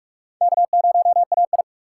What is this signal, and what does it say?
Morse code, heard as a keyed tone, about 41 wpm, 700 Hz. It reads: K0AI
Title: USA Call Signs - 4 Characters - 4x Character Spacing - 45wpm